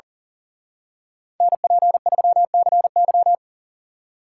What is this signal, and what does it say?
Morse code, heard as a keyed tone, about 38 wpm, 700 Hz. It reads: NP3CY
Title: Speed Racing - USA 2x2 Call Signs 40wpm